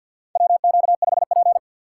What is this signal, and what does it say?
Morse code, heard as a keyed tone, about 52 wpm, 700 Hz. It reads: WQ5P
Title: USA Call Signs - 6x Character Spacing - 50wpm